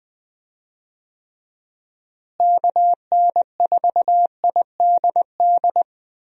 Morse code, heard as a keyed tone, about 20 wpm, 700 Hz. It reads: KN4IDD
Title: USA Call Signs - 6 Characters - Repeated 3x - 20wpm